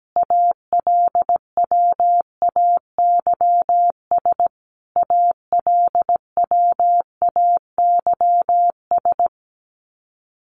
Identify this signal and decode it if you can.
Morse code, heard as a keyed tone, about 17 wpm, 700 Hz. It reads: ALWAYS ALWAYS
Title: Top 300 Words Repeated 17wpm